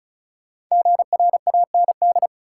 Morse code, heard as a keyed tone, about 36 wpm, 700 Hz. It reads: GRAND
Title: Five Letter Words - 35wpm